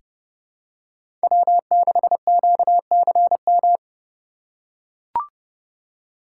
Morse code, heard as a keyed tone, about 31 wpm, 700 Hz. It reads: W6QCM
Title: USA Call Signs - 5 Characters - 4x Character Spacing - 30wpm